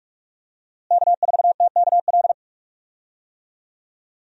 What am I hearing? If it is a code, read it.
Morse code, heard as a keyed tone, about 46 wpm, 700 Hz. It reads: K4TKL
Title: USA Call Signs - 5 Characters - Repeated 2x - 45wpm